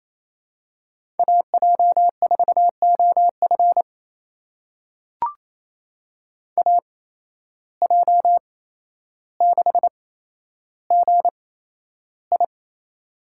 Morse code, heard as a keyed tone, about 28 wpm, spaced as for 12 wpm, 700 Hz. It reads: AJ4OF AJ6GI
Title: USA Call Signs - 5 Characters - 8x Character Spacing - 28wpm